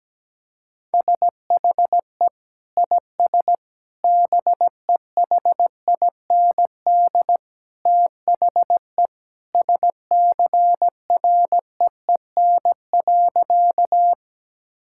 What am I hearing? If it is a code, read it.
Morse code, heard as a keyed tone, about 17 wpm, 700 Hz. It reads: SHE IS BEHIND THE SCREEN.
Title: Sentences from Top 1000 Words - Review Part 6 - 17wpm